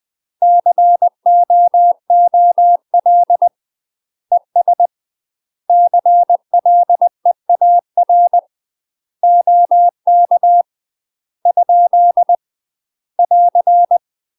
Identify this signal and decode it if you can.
Morse code, heard as a keyed tone, about 20 wpm, 700 Hz. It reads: COOL ES CLEAR OK ? <AR>
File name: UK MASTER QSOs TRAINING / 20WPM / UK MASTER QSOs 0059